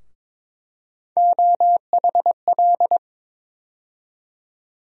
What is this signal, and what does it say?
Morse code, heard as a keyed tone, about 22 wpm, 700 Hz. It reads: OHL
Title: USA Call Sign Postfixes - 3 Letters 22wpm